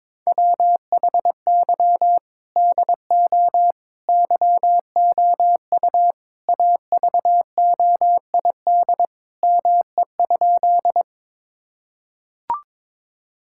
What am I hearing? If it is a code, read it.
Morse code, heard as a keyed tone, about 22 wpm, 700 Hz. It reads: WHY DO YOU AVOID ME?
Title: Sentences from Top 1000 Words - Review Part 6 - 22wpm